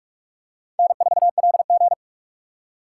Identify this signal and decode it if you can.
Morse code, heard as a keyed tone, about 44 wpm, 700 Hz. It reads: N4LG
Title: USA Call Signs - 4 Characters - 6x Character Spacing - 45wpm